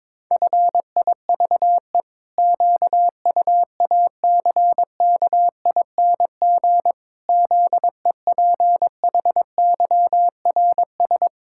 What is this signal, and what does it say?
Morse code, heard as a keyed tone, about 22 wpm, 700 Hz. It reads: FIVE QUACKING ZEPHYRS
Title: Pangram Challenge v2 - 22wpm